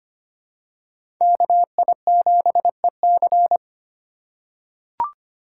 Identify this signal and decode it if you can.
Morse code, heard as a keyed tone, about 25 wpm, 700 Hz. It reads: KI7EC
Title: USA Call Signs - 6x Character Spacing - 25wpm